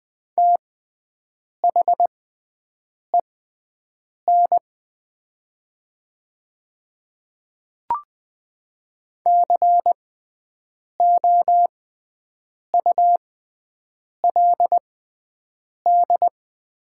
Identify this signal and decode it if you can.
Morse code, heard as a keyed tone, about 20 wpm, spaced as for 7 wpm, 700 Hz. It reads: THEN COULD
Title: 6x Character Spacing - Top 200 Words - 20wpm